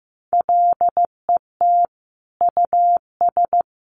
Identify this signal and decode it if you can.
Morse code, heard as a keyed tone, about 15 wpm, 700 Hz. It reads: LET US
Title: Sentences from Top 1400 Words 15wpm